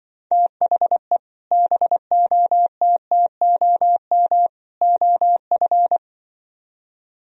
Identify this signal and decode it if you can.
Morse code, heard as a keyed tone, about 24 wpm, 700 Hz. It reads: THE BOTTOM OF
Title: Speed-Racing - Sets of 3 Words 20wpm